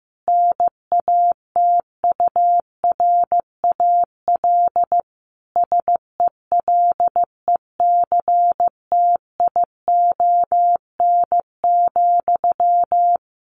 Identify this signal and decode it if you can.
Morse code, heard as a keyed tone, about 15 wpm, 700 Hz. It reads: NATURAL SELECTION,
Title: Sentences from Top 1400 Words 15wpm